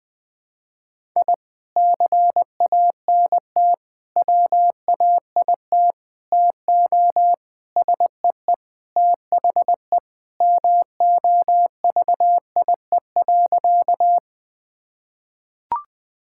Morse code, heard as a keyed tone, about 21 wpm, 700 Hz. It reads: I CANT WAIT TO SEE THE MOVIE.
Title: Sentences with 3-Word Phrases - Spoken Beginning - 20wpm